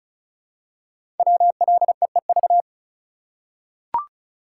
Morse code, heard as a keyed tone, about 36 wpm, 690 Hz. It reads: WLEEV
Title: Code Groups - 5 Letters - 35wpm